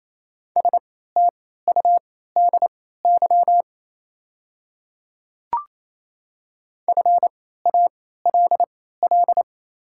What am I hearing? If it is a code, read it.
Morse code, heard as a keyed tone, about 28 wpm, spaced as for 16 wpm, 690 Hz. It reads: STUDY FALL
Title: 3x Character Spacing - Top 500 Words - 28wpm